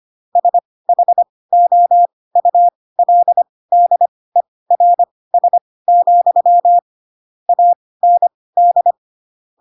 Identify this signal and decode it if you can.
Morse code, heard as a keyed tone, about 25 wpm, spaced as for 18 wpm, 700 Hz. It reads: SHOULDERS, AND
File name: Pinocchio / Pinocchio ch6 12 25 ik1hge